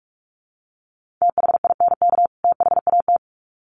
Morse code, heard as a keyed tone, about 44 wpm, 700 Hz. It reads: THINK THAT
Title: Sets of 2 Words 45wpm